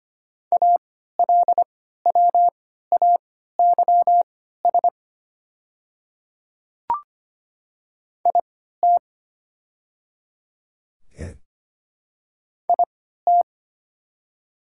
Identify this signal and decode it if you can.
Morse code, heard as a keyed tone, about 25 wpm, spaced as for 14 wpm, 690 Hz. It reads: ALWAYS IT IT
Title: 3x Character Spacing - Top 300 Words - 25wpm